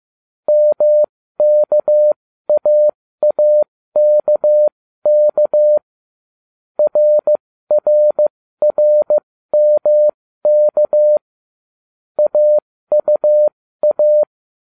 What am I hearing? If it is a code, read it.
Morse code, heard as a keyed tone, about 15 wpm, spaced as for 13 wpm, 600 Hz. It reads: MKAAKK RRRMK AUA